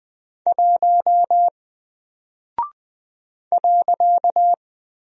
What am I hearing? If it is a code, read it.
Morse code, heard as a keyed tone, about 20 wpm, 700 Hz. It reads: 1 .
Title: Single Character (Letter:Number:Punctuation:Prosign) 20wpm